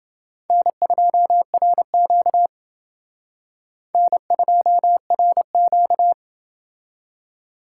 Morse code, heard as a keyed tone, about 30 wpm, 700 Hz. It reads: N2RQ N2RQ